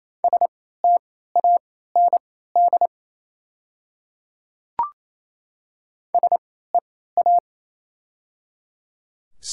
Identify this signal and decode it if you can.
Morse code, heard as a keyed tone, about 28 wpm, spaced as for 16 wpm, 700 Hz. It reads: STAND SEA
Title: 3x Character Spacing - Top 300 Words - 28wpm